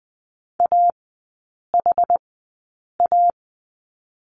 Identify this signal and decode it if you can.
Morse code, heard as a keyed tone, about 20 wpm, spaced as for 8 wpm, 700 Hz. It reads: AHA
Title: 060 Lesson 11e - Letter h - Three Characters 20wpm